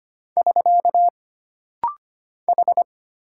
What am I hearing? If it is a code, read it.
Morse code, heard as a keyed tone, about 25 wpm, 700 Hz. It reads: <SK> H
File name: Single Character (Letter:Number:Punctuation:Prosign) - Rapid-Fire - 25wpm